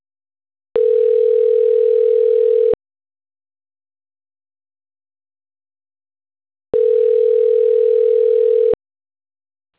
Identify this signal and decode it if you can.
Ringback tone (North America), 440 + 480 Hz.
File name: autotestagent / 11-31-02.136..callcentertransfer. .efae07a7-81c8-46e0-bd1f